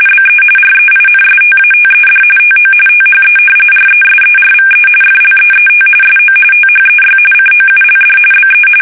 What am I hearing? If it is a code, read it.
RTTY, about 45.45 baud. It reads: BE REINSTATED TO THE RESERVE FLEET SO THEY COULD BE
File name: fsk850 / battleshipnj